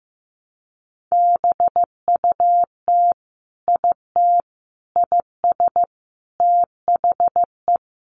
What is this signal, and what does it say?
Morse code, heard as a keyed tone, about 15 wpm, 700 Hz. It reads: BUT IT IS THE
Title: Sentences from Top 300 Words 15wpm